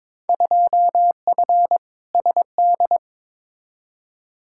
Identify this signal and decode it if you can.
Morse code, heard as a keyed tone, about 22 wpm, 700 Hz. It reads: 2F SD
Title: Field Day Exchanges Encore - 22wpm